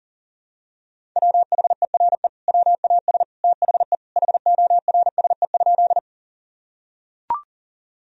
Morse code, heard as a keyed tone, about 40 wpm, 700 Hz. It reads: WHERE WAS THE HORSE?
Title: Sentences from Top 1000 Words - Review Part 1 - 40wpm